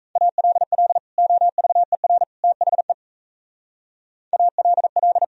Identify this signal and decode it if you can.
Morse code, heard as a keyed tone, about 33 wpm, 700 Hz. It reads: ALL OVER THE ALL
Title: Speed-Racing - Sets of 3 Words 28wpm